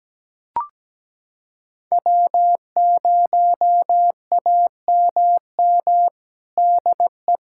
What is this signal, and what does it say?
Morse code, heard as a keyed tone, about 17 wpm, 700 Hz. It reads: W0AMM DE
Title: USA Callsign DE Callsign 17wpm